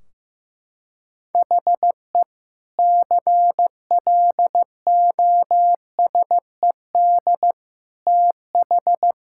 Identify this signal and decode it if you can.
Morse code, heard as a keyed tone, about 16 wpm, 700 Hz. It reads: HE CLOSED TH
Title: Sentences from Top 1600 Words 15wpm